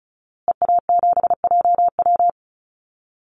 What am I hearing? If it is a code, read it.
Morse code, heard as a keyed tone, about 34 wpm, 700 Hz. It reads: EA7JW
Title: Call Signs Repeated 2x - 35wpm